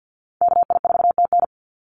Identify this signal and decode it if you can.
Morse code, heard as a keyed tone, about 48 wpm, 700 Hz. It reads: KI4TN
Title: NA SOTA Chaser Callsigns 50wpm